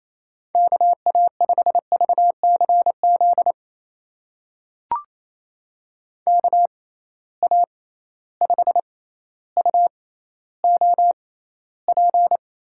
Morse code, heard as a keyed tone, about 28 wpm, spaced as for 14 wpm, 700 Hz. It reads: KA5VCZ KA5UOP
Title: USA Call Signs - 6 Characters - 6x Character Spacing - 28wpm